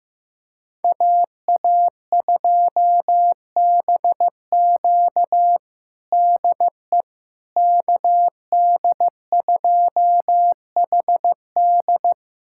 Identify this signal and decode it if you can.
Morse code, heard as a keyed tone, about 15 wpm, 700 Hz. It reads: AA2BQ DE KD2HD
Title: USA Callsign DE Callsign - 5 Characters - 15wpm